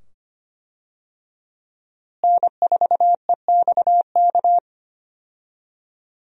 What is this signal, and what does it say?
Morse code, heard as a keyed tone, about 25 wpm, 700 Hz. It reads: N4EXK
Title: USA Call Signs - 5 Characters - Repeated 2x - 25wpm